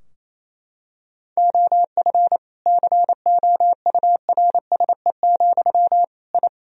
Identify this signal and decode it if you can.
Morse code, heard as a keyed tone, about 28 wpm, 700 Hz. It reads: OF COURSE, I
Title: Sentences from Top 1000 Words - Review Part 3 - 28wpm